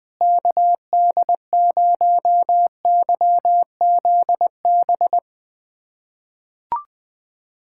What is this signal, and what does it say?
Morse code, heard as a keyed tone, about 20 wpm, 700 Hz. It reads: KD0YZB
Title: USA Callsign DE Callsign - 6 Characters - 20wpm